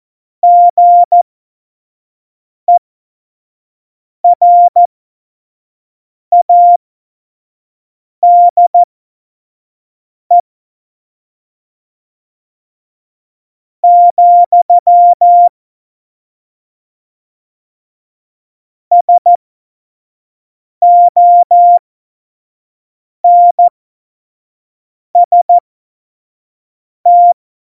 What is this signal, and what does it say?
Morse code, heard as a keyed tone, about 14 wpm, spaced as for 5 wpm, 700 Hz. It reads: GERADE, SONST
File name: Morsekurs / Witze / Witz 11. Dezember 2018, 25 F70